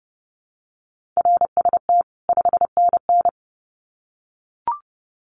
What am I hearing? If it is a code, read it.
Morse code, heard as a keyed tone, about 30 wpm, 700 Hz. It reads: RST 5NN
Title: Common QSO Phrases 30wpm